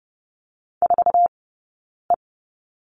Morse code, heard as a keyed tone, about 30 wpm, spaced as for 9 wpm, 690 Hz. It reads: 4E
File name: course / 30:10wpm / 086 Lesson 15d - Number 5 - Two Characters 30wpm